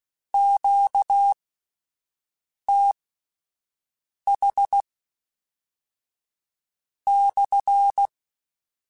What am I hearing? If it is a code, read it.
Morse code, heard as a keyed tone, about 16 wpm, spaced as for 5 wpm, 800 Hz. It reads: QTH /
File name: Q Calls / Q Calls-mp3s / Q CALLS SPSL @10